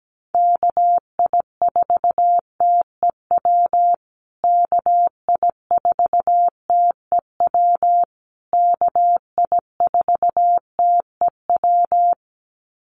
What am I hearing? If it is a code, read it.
Morse code, heard as a keyed tone, about 17 wpm, 700 Hz. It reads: KI4TEW KI4TEW KI4TEW
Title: USA Call Signs Repeated 3x 17wpm